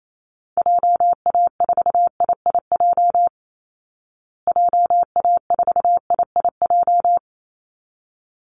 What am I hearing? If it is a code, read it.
Morse code, heard as a keyed tone, about 28 wpm, 700 Hz. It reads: JA4IIJ JA4IIJ
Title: Call Signs Repeated 3x - 28wpm